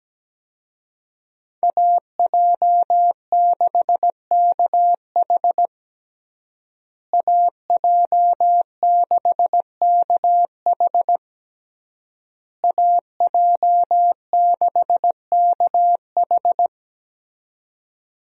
Morse code, heard as a keyed tone, about 17 wpm, 700 Hz. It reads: AJ6KH AJ6KH AJ6KH
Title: USA Call Signs - 5 Characters - Repeated 3x - 17wpm